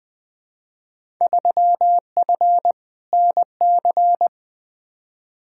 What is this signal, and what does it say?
Morse code, heard as a keyed tone, about 20 wpm, 700 Hz. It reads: 3F NC
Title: Field Day Exchanges Encore - 20wpm